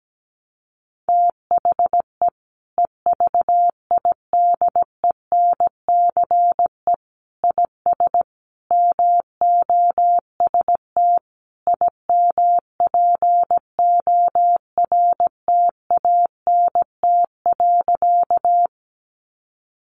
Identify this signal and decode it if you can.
Morse code, heard as a keyed tone, about 17 wpm, 700 Hz. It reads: THE EVIDENCE IS MOST IMPORTANT.
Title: Sentences from Top 1300 Words 17wpm